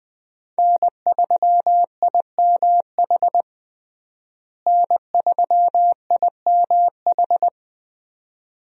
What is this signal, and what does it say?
Morse code, heard as a keyed tone, about 20 wpm, 700 Hz. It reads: N3IMH N3IMH
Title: USA Call Signs - 5 Characters - Repeated 2x - 20wpm